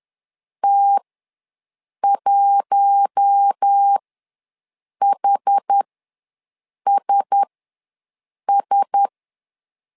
Morse code, heard as a keyed tone, about 10 wpm, spaced as for 6 wpm, 800 Hz. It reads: T1HSS
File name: LEZIONI / 4 Q1S ws 10wpm 2208